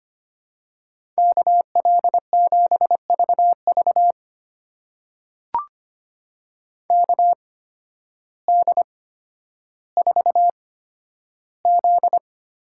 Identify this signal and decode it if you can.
Morse code, heard as a keyed tone, about 25 wpm, spaced as for 12 wpm, 700 Hz. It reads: KL7VV KD4Z
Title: USA Call Signs - 8x Character Spacing - 25wpm